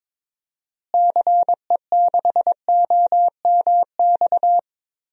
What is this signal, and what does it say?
Morse code, heard as a keyed tone, about 22 wpm, 700 Hz. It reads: CE6OMX
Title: DXCC-100 Call Signs - 6 Characters - Repeated 2x - 22wpm